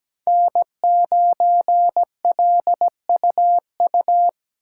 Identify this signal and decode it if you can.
Morse code, heard as a keyed tone, about 17 wpm, 700 Hz. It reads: N9LUU
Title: USA Call Signs - 5 Characters - 2x Character Spacing - 17wpm